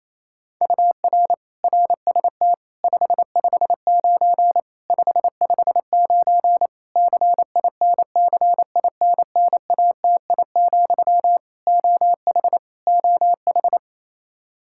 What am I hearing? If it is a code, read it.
Morse code, heard as a keyed tone, about 28 wpm, 700 Hz. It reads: UR RST 559 559 CINCINNATI, OH OH
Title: Practice QSO - Exchange 1 - Beginning - 28wpm